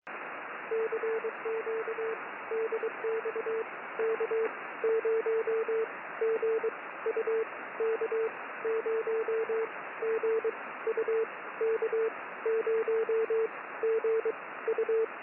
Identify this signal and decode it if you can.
Morse code, heard as a keyed tone, about 22 wpm, spaced as for 16 wpm, 450 Hz, with radio noise. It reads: CQ DX K0GU K0GU K0GU
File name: k0gu-6m-cw